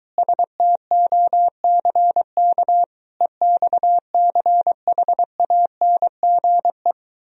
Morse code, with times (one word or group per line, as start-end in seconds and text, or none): 0.2-2.8 STOCK
3.2-6.9 EXCHANGE